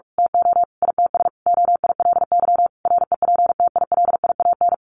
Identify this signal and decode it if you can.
Morse code, heard as a keyed tone, about 44 wpm, 700 Hz. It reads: TO ITS OILY REPTILIAN
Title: A Sound Of Thunder 45wpm